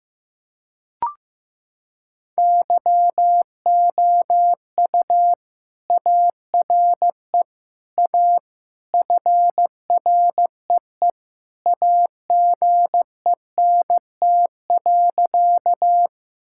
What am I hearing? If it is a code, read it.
Morse code, heard as a keyed tone, about 15 wpm, 700 Hz. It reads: YOU ARE A FREE AGENT.
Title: Sentences from Top 1600 Words 15wpm